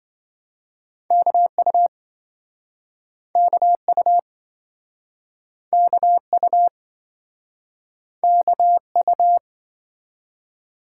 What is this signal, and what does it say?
Morse code, heard as a keyed tone, about 28 wpm, 700 Hz. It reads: KU KU KU KU